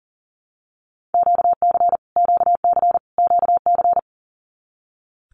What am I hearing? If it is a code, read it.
Morse code, heard as a keyed tone, about 40 wpm, 700 Hz. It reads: QC QC QC
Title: Canadian Provinces and Territories - Repeated 3x - 40wpm